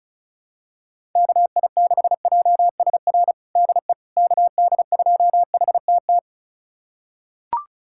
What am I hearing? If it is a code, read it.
Morse code, heard as a keyed tone, about 34 wpm, 700 Hz. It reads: KI6JSR DE KD2HTT
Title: USA Callsign DE Callsign - 6 Characters - 35wpm